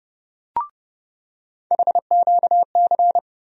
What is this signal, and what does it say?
Morse code, heard as a keyed tone, about 30 wpm, 700 Hz. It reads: HQC
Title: USA Call Sign Postfixes - 3 Letters 30wpm